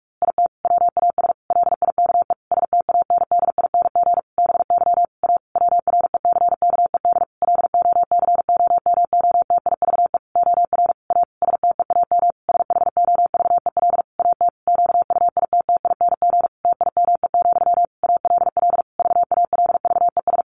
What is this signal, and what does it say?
Morse code, heard as a keyed tone, about 44 wpm, 700 Hz. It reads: IT WAS LIKE STANDING BY A WRECKED LOCOMOTIVE OR A STEAM SHOVEL AT QUITTING TIME, ALL VALVES
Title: A Sound Of Thunder 45wpm